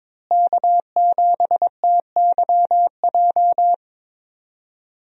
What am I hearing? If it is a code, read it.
Morse code, heard as a keyed tone, about 22 wpm, 700 Hz. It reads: K7TYJ
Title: USA Call Signs - 5 Characters - Repeated 2x - 22wpm